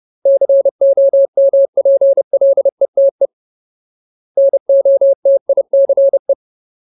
Morse code, heard as a keyed tone, about 30 wpm, 550 Hz. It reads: COMPLETE NOTICE
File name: Ultimate Word List / 30WPM Ultimate